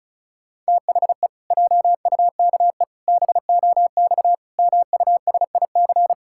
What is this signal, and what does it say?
Morse code, heard as a keyed tone, about 36 wpm, 700 Hz. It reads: THE JUKE BOX MUSIC
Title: Pangram Challenge v2 - 35wpm